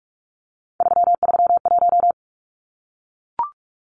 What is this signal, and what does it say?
Morse code, heard as a keyed tone, about 44 wpm, 700 Hz. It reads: 331
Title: Numbers - 3-Digit 45wpm